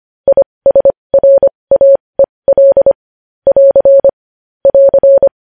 Morse code, heard as a keyed tone, about 25 wpm, spaced as for 20 wpm, 550 Hz. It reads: ISRAEL <AR> <AR>
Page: News Headlines in Morse Code at 20 WPM